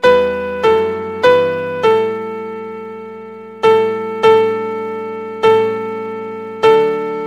There is no code.